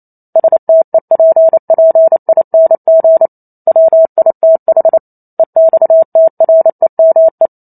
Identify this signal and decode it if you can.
Morse code, heard as a keyed tone, about 28 wpm, 650 Hz. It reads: STEPPING WITH EXTREME